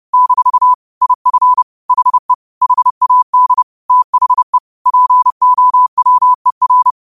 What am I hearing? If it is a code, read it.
Morse code, heard as a keyed tone, about 30 wpm, 1000 Hz. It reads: - IF HE HAD THE POWER
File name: qotdfiles / 30wpm